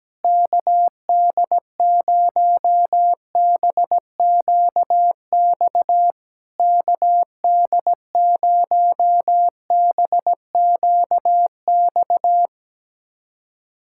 Morse code, heard as a keyed tone, about 17 wpm, 660 Hz. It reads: KD0BQX KD0BQX
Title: SOTA QSO Practice 17wpm